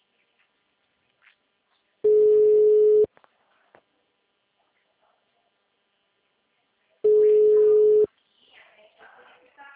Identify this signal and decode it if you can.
Ringback tone (Europe, 425 Hz), with speech.
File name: out- -2006- - - .69